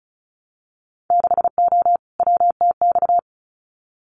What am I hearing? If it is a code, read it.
Morse code, heard as a keyed tone, about 36 wpm, 700 Hz. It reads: 6O WTX
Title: Winter Field Day Exchanges Encore 35wpm